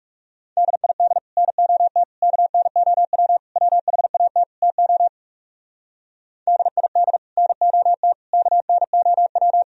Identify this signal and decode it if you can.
Morse code, heard as a keyed tone, about 36 wpm, 700 Hz. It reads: DID NOT KNOW WHAT TO DID NOT KNOW
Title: Speed-Racing - Sets of 5 Words 30wpm